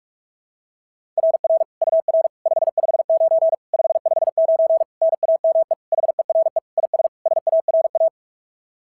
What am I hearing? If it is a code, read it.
Morse code, heard as a keyed tone, about 43 wpm, 660 Hz. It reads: RR UR 559 559 NAME HERE IS SARA
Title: SOTA QSO Practice 45wpm